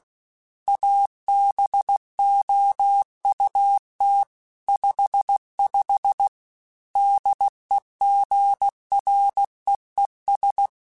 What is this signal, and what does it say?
Morse code, heard as a keyed tone, about 16 wpm, 800 Hz. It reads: ABOUT 55 DEGREES
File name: Weather Sets / Weather Sets-mp3s / Weather 4 24 @15